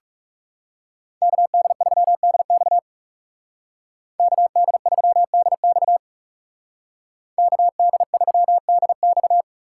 Morse code, heard as a keyed tone, about 40 wpm, 700 Hz. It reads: KD3DX KD3DX KD3DX